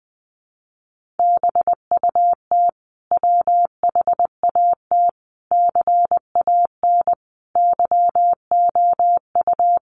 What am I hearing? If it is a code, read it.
Morse code, heard as a keyed tone, about 19 wpm, 700 Hz. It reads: BUT WHAT CAN YOU
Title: Sentences from Top 1100 Words 20wpm